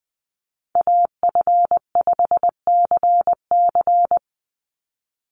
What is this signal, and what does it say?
Morse code, heard as a keyed tone, about 20 wpm, 700 Hz. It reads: AF5CC
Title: Call Signs - 20wpm